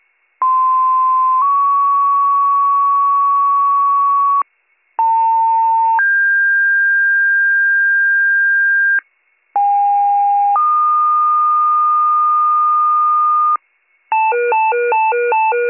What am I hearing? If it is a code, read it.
Two-tone page, heard as a one-second tone, then a three-second tone, about 1040 then 1120 Hz, about 900 then 1600 Hz, about 810 then 1220 Hz. It is heard from 0.4 to 13.6 s.